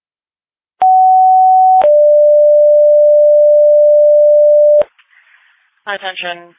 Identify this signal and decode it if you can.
Two-tone page, heard as a one-second tone, then a three-second tone, about 750 then 590 Hz, with speech. It is heard from 0.8 to 4.8 s.
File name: Readlyn Fire